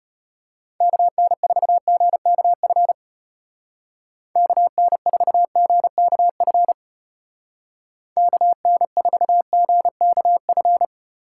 Morse code, heard as a keyed tone, about 32 wpm, 700 Hz. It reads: KN4GKF KN4GKF KN4GKF